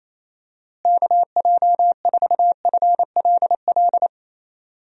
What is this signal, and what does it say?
Morse code, heard as a keyed tone, about 28 wpm, 700 Hz. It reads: KJ4FLL
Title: USA Call Signs - 6 Characters - Repeated 3x - 28wpm